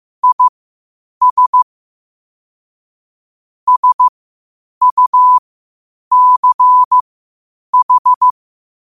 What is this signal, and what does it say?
Morse code, heard as a keyed tone, about 15 wpm, spaced as for 8 wpm, 1000 Hz. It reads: IS SUCH